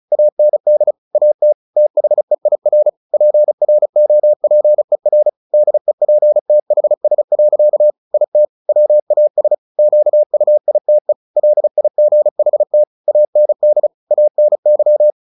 Morse code, heard as a keyed tone, about 34 wpm, 600 Hz. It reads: AND AT THEIR PROPER DEPTHS. IT WAS QUITE LIGHT AND ANY
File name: the old man and the sea / the old man and the sea 1 35wpm